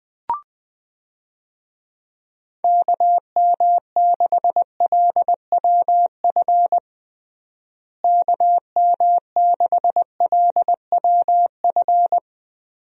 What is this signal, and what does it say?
Morse code, heard as a keyed tone, about 21 wpm, 700 Hz. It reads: KM6LWF KM6LWF
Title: USA Call Signs - 6 Characters - Repeated 2x - 20wpm